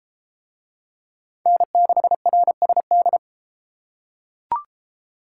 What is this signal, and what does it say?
Morse code, heard as a keyed tone, about 30 wpm, 700 Hz. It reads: N6RSD